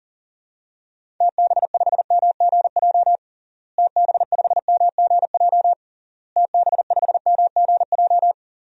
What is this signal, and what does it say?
Morse code, heard as a keyed tone, about 40 wpm, 700 Hz. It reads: TB5MGJ TB5MGJ TB5MGJ
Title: DXCC-100 Call Signs - 6 Characters - Repeated 3x - 40wpm